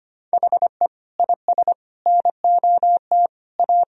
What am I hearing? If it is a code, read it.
Morse code, heard as a keyed tone, about 25 wpm, 700 Hz. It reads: HE IS NOT A
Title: Sentences from Top 1000 Words - Review Part 2 - 25wpm